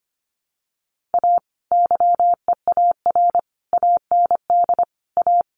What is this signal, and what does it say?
Morse code, heard as a keyed tone, about 25 wpm, 700 Hz. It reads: A YEAR AND A
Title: Sets of 5 Words 25wpm